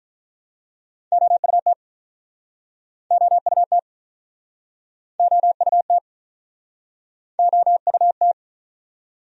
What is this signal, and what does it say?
Morse code, heard as a keyed tone, about 47 wpm, 700 Hz. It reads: OUT OUT OUT OUT